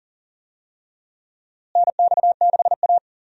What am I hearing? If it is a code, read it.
Morse code, heard as a keyed tone, about 40 wpm, 700 Hz. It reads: NX6A
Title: USA Call Signs - 4 Characters - Repeated 3x - 40wpm